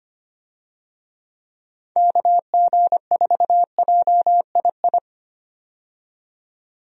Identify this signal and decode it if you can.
Morse code, heard as a keyed tone, about 25 wpm, 700 Hz. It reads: KG4JII
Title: USA Call Signs - 6 Characters - 25wpm